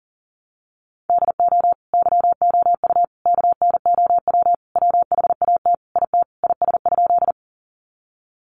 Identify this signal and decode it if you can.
Morse code, heard as a keyed tone, about 40 wpm, 700 Hz. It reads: DO YOU KNOW WHAT IT IS?
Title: Sentences from Top 100 Words 40wpm